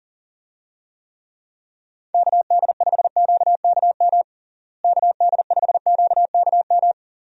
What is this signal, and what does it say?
Morse code, heard as a keyed tone, about 40 wpm, 700 Hz. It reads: KD5QKM KD5QKM
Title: USA Call Signs - 6 Characters - Repeated 3x - 40wpm